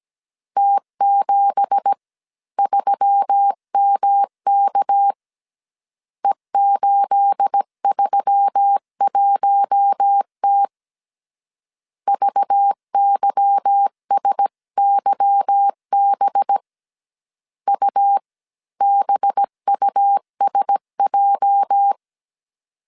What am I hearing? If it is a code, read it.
Morse code, heard as a keyed tone, about 17 wpm, 800 Hz. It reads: T73MK E831T VYSYB UBUSJ
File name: CW / DET / 1 9 1500C S B 16wpm 2856